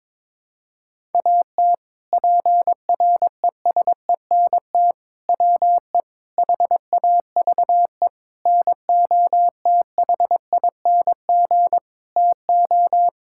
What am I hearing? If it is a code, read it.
Morse code, heard as a keyed tone, about 22 wpm, 700 Hz. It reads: AT PRESENT WE HAVE NOTHING TO
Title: Sentences from Top 1000 Words - Review Part 5 - 22wpm